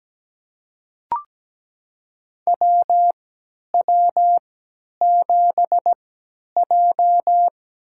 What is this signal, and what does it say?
Morse code, heard as a keyed tone, about 17 wpm, spaced as for 10 wpm, 700 Hz. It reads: WW7J
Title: USA Call Signs - 4 Characters - 3x Character Spacing - 17wpm